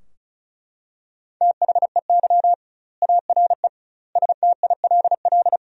Morse code, heard as a keyed tone, about 34 wpm, 700 Hz. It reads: THEY ARE STILL
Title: 2x Word-Spacing - Sentences from Top 400 Words - 35wpm